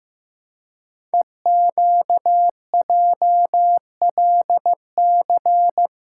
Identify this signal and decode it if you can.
Morse code, heard as a keyed tone, about 15 wpm, 700 Hz. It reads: EQJLC
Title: Code Groups - 5 Letter:Numbers - 15wpm